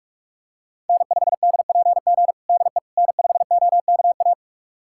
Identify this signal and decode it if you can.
Morse code, heard as a keyed tone, about 47 wpm, 700 Hz. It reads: N5DPG DE N5OKA
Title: USA Callsign DE Callsign - 5 Characters - 45wpm